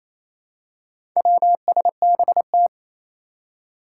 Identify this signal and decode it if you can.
Morse code, heard as a keyed tone, about 28 wpm, 700 Hz. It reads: WSBT
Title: Four Challenging Letter-Number Combinations 28wpm